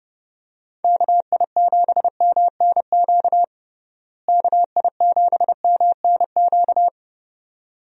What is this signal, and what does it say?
Morse code, heard as a keyed tone, about 30 wpm, 700 Hz. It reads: KI7MNQ KI7MNQ
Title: USA Call Signs - 6 Characters - Repeated 2x - 30wpm